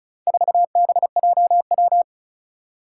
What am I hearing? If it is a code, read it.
Morse code, heard as a keyed tone, about 36 wpm, 700 Hz. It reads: 4BJW
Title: Four Challenging Letter-Number Combinations 35wpm